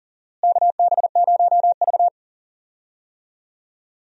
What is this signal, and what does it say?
Morse code, heard as a keyed tone, about 40 wpm, 700 Hz. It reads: KB0V
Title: USA Call Signs - 4 Characters - Repeated 3x - 40wpm